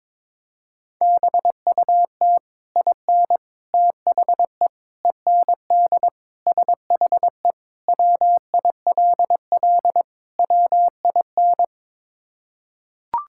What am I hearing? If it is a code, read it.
Morse code, heard as a keyed tone, about 22 wpm, 700 Hz. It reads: BUT IN THE END SHE WILL WIN
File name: practice / 22wpm / Sentences with 3-Word Phrases - 22wpm